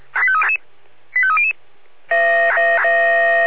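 Five-tone selective call (ZVEI), heard as quick runs of five tones, tones of about 70 ms. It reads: 85300
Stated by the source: ZVEI five-tone selective call received on SDR.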